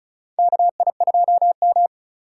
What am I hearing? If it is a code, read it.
Morse code, heard as a keyed tone, about 34 wpm, 700 Hz. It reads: KI2M
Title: USA Call Signs - 4x Character Spacing - 35wpm